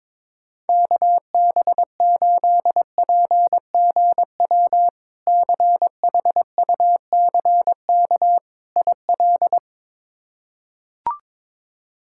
Morse code, heard as a keyed tone, about 22 wpm, 700 Hz. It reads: KB8PGW CHUCK IL
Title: K1USN SST QSO Practice - 22wpm